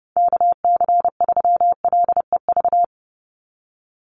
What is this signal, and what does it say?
Morse code, heard as a keyed tone, about 28 wpm, 700 Hz. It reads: KC3LEV
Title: USA Call Signs 30wpm